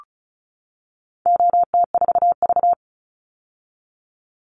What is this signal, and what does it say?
Morse code, heard as a keyed tone, about 34 wpm, 700 Hz. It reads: OT4V
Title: Call Signs - 35wpm